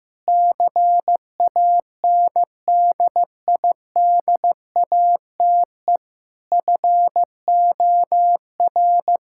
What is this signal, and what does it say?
Morse code, heard as a keyed tone, about 15 wpm, 700 Hz. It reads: CANDIDATE FOR
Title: Sentences from Top 1600 Words 15wpm